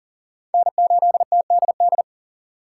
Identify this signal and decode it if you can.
Morse code, heard as a keyed tone, about 52 wpm, 700 Hz. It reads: N8TDD